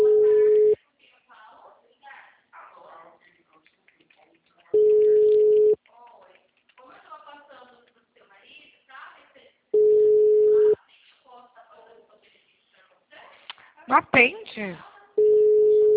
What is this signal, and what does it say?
Ringback tone (Europe, 425 Hz), with speech.
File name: out- -2009- - - .63379